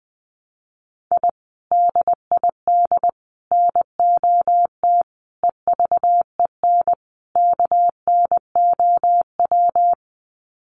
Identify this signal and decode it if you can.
Morse code, heard as a keyed tone, about 20 wpm, 700 Hz. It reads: I DID NOT EVEN KNOW
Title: Sets of 5 Words Encore 20wpm